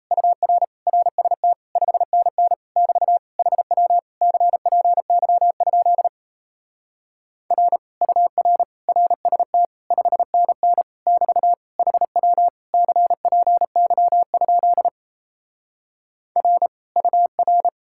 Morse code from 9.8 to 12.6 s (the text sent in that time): 5NN <BT> HW